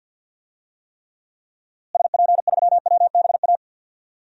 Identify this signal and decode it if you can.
Morse code, heard as a keyed tone, about 48 wpm, 700 Hz. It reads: SP3WBA
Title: DXCC-100 Call Signs - 6 Characters - Repeated 2x - 50wpm